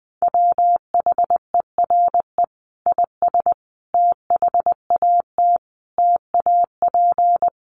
Morse code, heard as a keyed tone, about 20 wpm, 700 Hz. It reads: WHERE IS THAT TAP
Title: Sentences from Top 1400 Words 20wpm